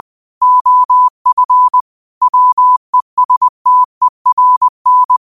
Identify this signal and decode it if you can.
Morse code, heard as a keyed tone, about 20 wpm, 1000 Hz. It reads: OF WESTERN